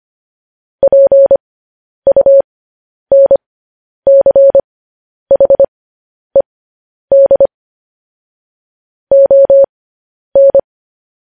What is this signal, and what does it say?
Morse code, heard as a keyed tone, about 25 wpm, spaced as for 10 wpm, 550 Hz. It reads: PUNCHED ON
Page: news headlines in morse code at 10 wpm